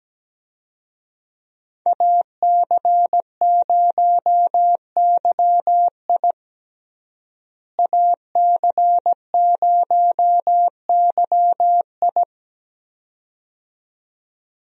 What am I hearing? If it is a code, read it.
Morse code, heard as a keyed tone, about 17 wpm, 700 Hz. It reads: AC0YI AC0YI
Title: USA Call Signs - 5 Characters - Repeated 2x - 17wpm